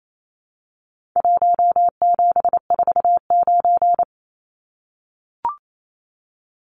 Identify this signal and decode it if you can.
Morse code, heard as a keyed tone, about 28 wpm, 700 Hz. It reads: 1749
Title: Numbers - Contest Style - 4-Digit 28wpm